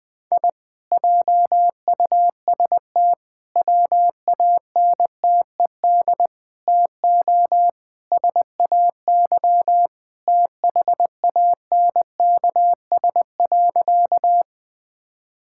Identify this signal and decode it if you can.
Morse code, heard as a keyed tone, about 20 wpm, 700 Hz. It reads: I JUST WANTED TO SAY THANKS.
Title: Sentences with 4-Word Phrases - 20wpm